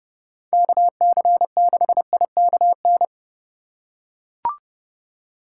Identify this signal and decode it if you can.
Morse code, heard as a keyed tone, about 30 wpm, 700 Hz. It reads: KC6IKN